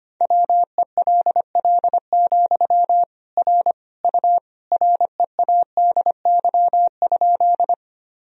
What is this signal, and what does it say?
Morse code, heard as a keyed tone, about 25 wpm, 700 Hz. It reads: WELL, R U READY?
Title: Sentences from Top 400 Words and Abbreviations 25wpm